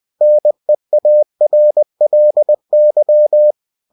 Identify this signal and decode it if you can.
Morse code, heard as a keyed tone, about 20 wpm, 600 Hz. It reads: NEARLY
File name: cw-books / winnie-the-pooh / winnie-20wpm-page-0065